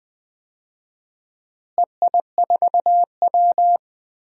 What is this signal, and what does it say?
Morse code, heard as a keyed tone, about 21 wpm, 700 Hz. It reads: EI4W
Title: DXCC-100 Call Signs - 4 Characters - Repeated 3x - 20wpm